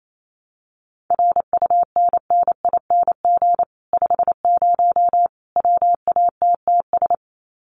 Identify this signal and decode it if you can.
Morse code, heard as a keyed tone, about 28 wpm, 700 Hz. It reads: RUNNING 50 WATTS
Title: Common QSO Phrases 28wpm